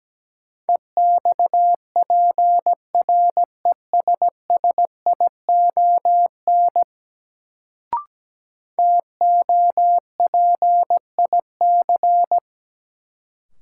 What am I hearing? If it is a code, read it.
Morse code, heard as a keyed tone, about 17 wpm, 700 Hz. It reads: EXPRESSION TOPIC
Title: Sentences from Top 1200 Words - New Word Drill - 17wpm